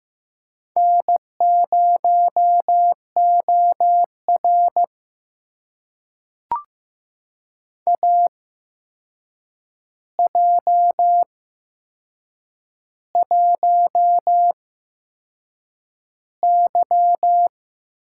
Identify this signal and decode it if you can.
Morse code, heard as a keyed tone, about 15 wpm, spaced as for 6 wpm, 700 Hz. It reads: N0OR AJ1Y
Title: USA Call Signs - 4 Characters - 8x Character Spacing - 15wpm